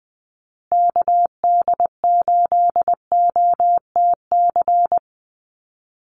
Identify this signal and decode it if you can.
Morse code, heard as a keyed tone, about 20 wpm, 700 Hz. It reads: KD8OTC
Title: USA Call Signs Repeated 3x 20wpm